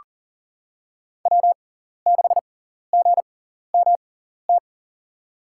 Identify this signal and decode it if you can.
Morse code, heard as a keyed tone, about 40 wpm, spaced as for 14 wpm, 700 Hz. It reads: W6GMT
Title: USA Call Signs - 6x Character Spacing - 40wpm